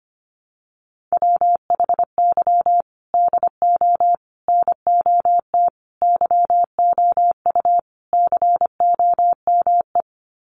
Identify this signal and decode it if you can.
Morse code, heard as a keyed tone, about 25 wpm, 700 Hz. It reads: WHY DO NOT YOU COME
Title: Sets of 5 Words Encore2 25wpm